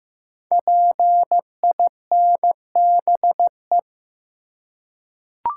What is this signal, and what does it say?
Morse code, heard as a keyed tone, about 15 wpm, 700 Hz. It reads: PINBE
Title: Code Groups - 5 Letter:Numbers - 15wpm